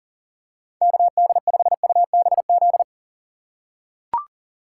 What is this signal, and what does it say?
Morse code, heard as a keyed tone, about 38 wpm, 700 Hz. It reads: KD5UBZ
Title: USA Call Signs - 6 Characters - 2x Character Spacing - 40wpm